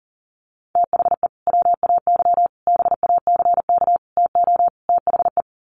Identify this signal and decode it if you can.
Morse code, heard as a keyed tone, about 40 wpm, 700 Hz. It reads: THE WAY BACK TO THE
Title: Sets of 5 Words Encore2 40wpm